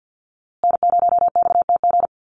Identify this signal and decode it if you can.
Morse code, heard as a keyed tone, about 54 wpm, 700 Hz. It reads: N0XTG
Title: USA Call Signs Repeated 3x 50wpm